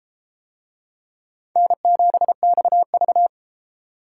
Morse code, heard as a keyed tone, about 31 wpm, 700 Hz. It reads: N7XV